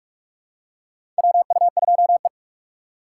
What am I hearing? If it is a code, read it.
Morse code, heard as a keyed tone, about 46 wpm, 700 Hz. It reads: WU2E
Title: USA Call Signs - 4 Characters - 4x Character Spacing - 45wpm